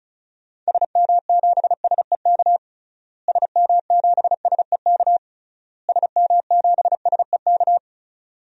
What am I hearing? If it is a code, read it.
Morse code, heard as a keyed tone, about 36 wpm, 700 Hz. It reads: SM7SEK SM7SEK SM7SEK
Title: DXCC-100 Call Signs - 6 Characters - Repeated 3x - 35wpm